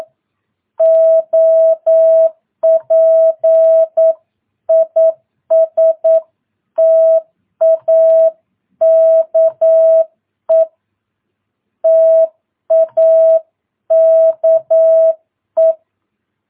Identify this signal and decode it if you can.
Morse code, heard as a keyed tone, about 9 wpm, 650 Hz. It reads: OP IS TAKE TAKE
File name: jh8cbh / 2022cw